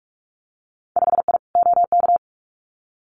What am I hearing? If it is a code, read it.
Morse code, heard as a keyed tone, about 46 wpm, 700 Hz. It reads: 5I OK
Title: Winter Field Day Exchanges Encore 45wpm